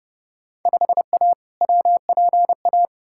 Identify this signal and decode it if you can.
Morse code, heard as a keyed tone, about 30 wpm, 700 Hz. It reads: 5A WPA
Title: Field Day Exchanges Encore - 30wpm